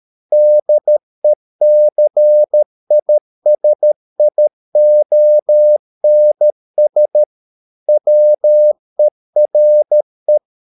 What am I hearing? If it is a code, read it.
Morse code, heard as a keyed tone, about 13 wpm, 600 Hz. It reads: DECISIONS WERE